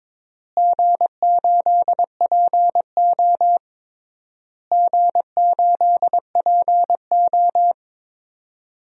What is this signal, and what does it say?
Morse code, heard as a keyed tone, about 22 wpm, 700 Hz. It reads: G8PO G8PO
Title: DXCC-100 Call Signs - 4 Characters - Repeated 3x - 22wpm